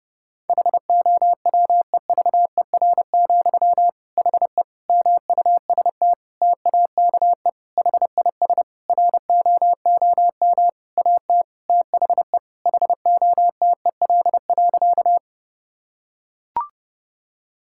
Morse code, heard as a keyed tone, about 30 wpm, 700 Hz. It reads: HOWEVER, HE MUST TAKE HIS ROOM AT THE HOTEL.
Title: Sentences from Top 1600 Words 30wpm